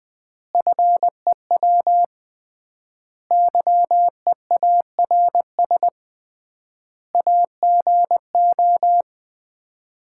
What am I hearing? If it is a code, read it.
Morse code, heard as a keyed tone, about 20 wpm, 700 Hz. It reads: FEW YEARS AGO
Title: 3x Word-Spacing - Sets of 4 Words - 20wpm